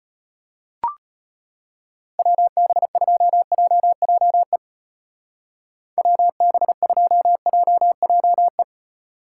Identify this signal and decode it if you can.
Morse code, heard as a keyed tone, about 32 wpm, 700 Hz. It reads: WB2JJE WB2JJE